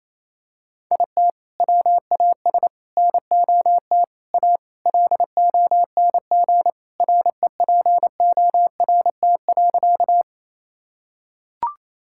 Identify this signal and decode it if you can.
Morse code, heard as a keyed tone, about 28 wpm, 700 Hz. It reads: IT WAS NOT A LONG REPORT.
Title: Sentences from Top 1000 Words - Review Part 6 - 28wpm